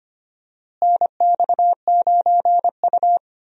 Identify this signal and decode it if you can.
Morse code, heard as a keyed tone, about 25 wpm, 700 Hz. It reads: NX9U
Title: USA Call Signs - 4 Characters - Repeated 3x - 25wpm